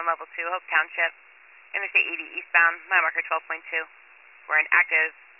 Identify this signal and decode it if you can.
Emergency Alert System (SAME) tones (header bursts), heard as harsh buzzing bursts, with speech. It is heard from 0.7 to 5.1 s.